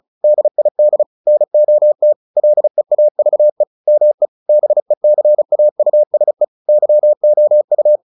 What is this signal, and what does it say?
Morse code, heard as a keyed tone, about 34 wpm, 600 Hz. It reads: DID NOT LEAVE ME BECAUSE YOU
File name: the old man and the sea / the old man and the sea 1 35wpm